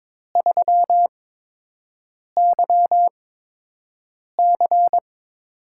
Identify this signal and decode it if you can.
Morse code, heard as a keyed tone, about 22 wpm, spaced as for 6 wpm, 700 Hz. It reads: 3YC
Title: USA Call Signs - 4 Characters - 8x Character Spacing - 22wpm